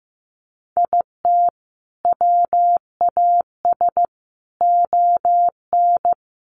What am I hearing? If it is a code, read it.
Morse code, heard as a keyed tone, about 15 wpm, 700 Hz. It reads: IT WAS ON
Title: Sentences from Top 800 Words 15wpm